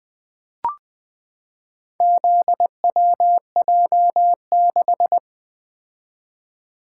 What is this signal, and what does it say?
Morse code, heard as a keyed tone, about 20 wpm, 700 Hz. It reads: ZWJ6
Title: Four Letter-Number Combinations - 20wpm